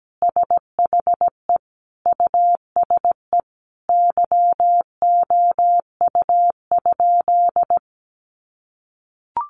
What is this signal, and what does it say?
Morse code, heard as a keyed tone, about 17 wpm, 700 Hz. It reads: SHE USE YOU?
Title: Sentences from Top 200 Words 17wpm